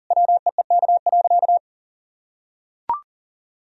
Morse code, heard as a keyed tone, about 37 wpm, 700 Hz. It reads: WEEK.
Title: Sentences from Top 1000 Words - Review Part 4 - 40wpm